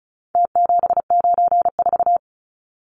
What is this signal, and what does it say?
Morse code, heard as a keyed tone, about 34 wpm, 700 Hz. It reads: T794
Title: Numbers - Contest Style - 4-Digit 35wpm